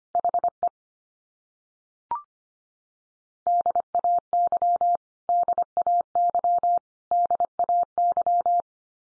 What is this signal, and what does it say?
Morse code, heard as a keyed tone, about 25 wpm, 700 Hz. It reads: HE DAY DAY DAY
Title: Top 100 Words 25wpm x3